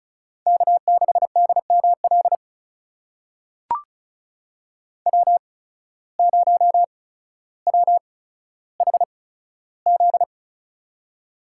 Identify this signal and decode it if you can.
Morse code, heard as a keyed tone, about 34 wpm, spaced as for 15 wpm, 700 Hz. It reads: K6DML W0WHZ
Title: USA Call Signs - 8x Character Spacing - 35wpm